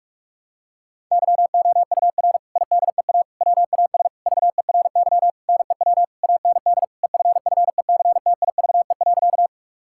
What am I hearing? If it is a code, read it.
Morse code, heard as a keyed tone, about 46 wpm, 700 Hz. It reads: YOUR IDEA WAS VERY NEW AND EFFECTIVE.
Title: Sentences from Top 1000 Words - Review Part 6 - 45wpm